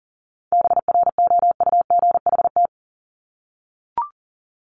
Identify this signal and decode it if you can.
Morse code, heard as a keyed tone, about 40 wpm, 700 Hz. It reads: BROUGHT
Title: Top 500 Words 40wpm